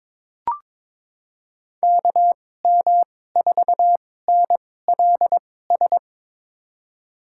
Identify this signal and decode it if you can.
Morse code, heard as a keyed tone, about 22 wpm, spaced as for 16 wpm, 700 Hz. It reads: KM4NLS
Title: USA Call Signs - 6 Characters - 2x Character Spacing - 22wpm